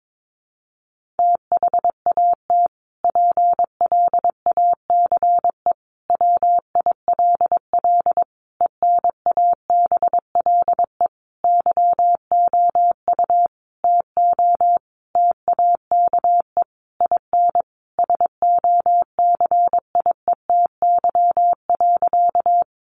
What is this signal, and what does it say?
Morse code, heard as a keyed tone, about 22 wpm, 700 Hz. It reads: THAT PLACE WILL ENABLE YOU TO TAKE IN SOCIETY.
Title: Sentences from Top 1500 Words 22wpm